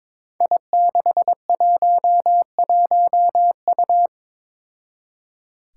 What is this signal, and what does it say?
Morse code, heard as a keyed tone, about 22 wpm, 700 Hz. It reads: I611U
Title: Five Challenging Letter-Number Combinations 22wpm